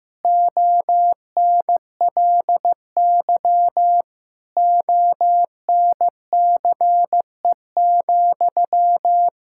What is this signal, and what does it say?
Morse code, heard as a keyed tone, about 15 wpm, 700 Hz. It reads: ONLY ONCE,
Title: Sentences from Top 1600 Words 15wpm